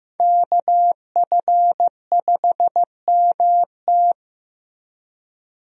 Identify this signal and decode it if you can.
Morse code, heard as a keyed tone, about 15 wpm, 700 Hz. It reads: KF5MT
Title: USA Call Signs - 5 Characters - 15wpm